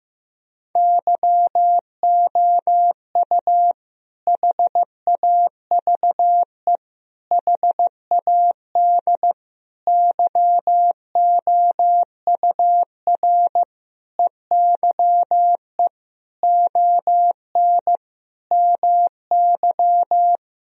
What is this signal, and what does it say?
Morse code, heard as a keyed tone, about 15 wpm, 700 Hz. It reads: YOU HAVE HAD YOUR EYE ON MY
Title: Sentences from Top 1000 Words - Review Part 1 - 15wpm